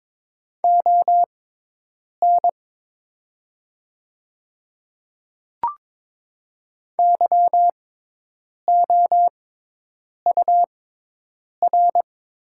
Morse code, heard as a keyed tone, about 22 wpm, spaced as for 8 wpm, 700 Hz. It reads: ON YOUR
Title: 6x Character Spacing - Top 100 Words - 22wpm